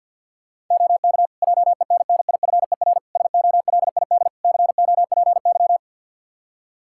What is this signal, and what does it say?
Morse code, heard as a keyed tone, about 52 wpm, 700 Hz. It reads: OK JENNIFER SOLID COPY
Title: Practice QSO - Exchange 3 - More Info - 50wpm